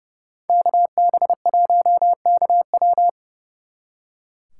Morse code, heard as a keyed tone, about 39 wpm, 700 Hz. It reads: KB1KW